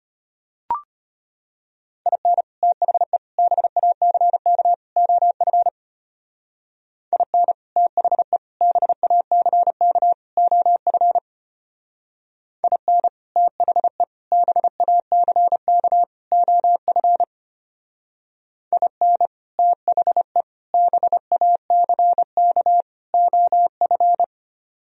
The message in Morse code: IN THE BACK OF IN THE BACK OF IN THE BACK OF IN THE BACK OF

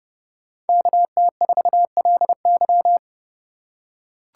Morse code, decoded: KT4LY

